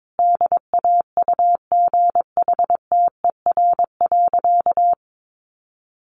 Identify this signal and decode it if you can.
Morse code, heard as a keyed tone, about 22 wpm, 700 Hz. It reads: DAUGHTER.
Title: Sentences from Top 1500 Words 22wpm